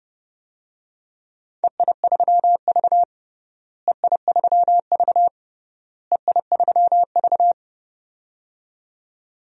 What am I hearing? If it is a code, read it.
Morse code, heard as a keyed tone, about 30 wpm, 700 Hz. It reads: EI3V EI3V EI3V
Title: DXCC-100 Call Signs - 4 Characters - Repeated 3x - 30wpm